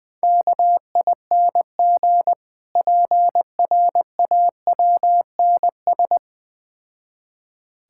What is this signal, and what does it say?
Morse code, heard as a keyed tone, about 20 wpm, 690 Hz. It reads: KING PRAWNS
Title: Difficult Words with Hints - 20wpm